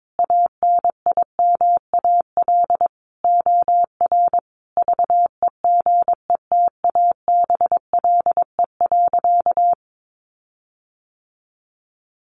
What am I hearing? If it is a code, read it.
Morse code, heard as a keyed tone, about 22 wpm, 700 Hz. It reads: ANIMAL OR VEGETABLE.
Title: Sentences from Top 1300 Words 22wpm